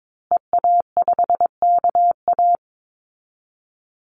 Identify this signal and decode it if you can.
Morse code, heard as a keyed tone, about 22 wpm, 700 Hz. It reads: EA5KA
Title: Call Signs Repeated 2x - 22wpm